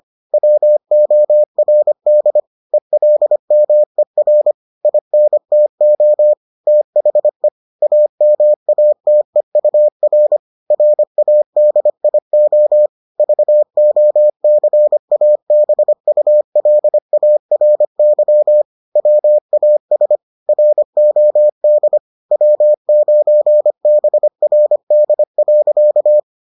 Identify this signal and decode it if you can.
Morse code, heard as a keyed tone, about 25 wpm, 600 Hz. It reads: WORD ELMER INTO THE AMATEUR RADIO VOCABULARY WAS ROD W9BRD.